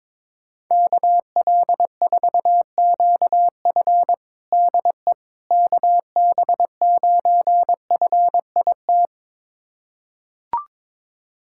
Morse code, heard as a keyed tone, about 22 wpm, 700 Hz. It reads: KL4QF DE KB9FIT
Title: USA Callsign DE Callsign 22wpm